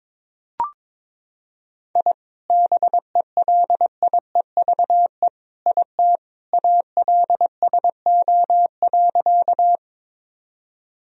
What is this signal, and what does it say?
Morse code, heard as a keyed tone, about 22 wpm, 700 Hz. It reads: I BELIEVE IT ALSO.
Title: Sentences from Top 1000 Words - Review Part 1 - 22wpm